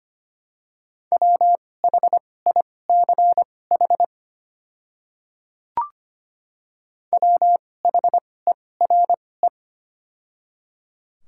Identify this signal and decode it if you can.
Morse code, heard as a keyed tone, about 25 wpm, spaced as for 18 wpm, 700 Hz. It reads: WHICH WHERE
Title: 2x Character Spacing - Top 500 Words - 25wpm